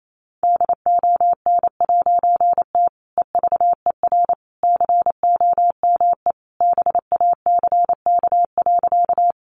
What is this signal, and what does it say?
Morse code, heard as a keyed tone, about 28 wpm, 700 Hz. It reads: DON'T EVER COME BACK.
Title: A Sound Of Thunder 28wpm